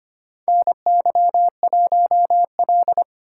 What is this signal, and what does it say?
Morse code, heard as a keyed tone, about 25 wpm, 700 Hz. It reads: NY1L
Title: USA Call Signs - 4 Characters - 4x Character Spacing - 25wpm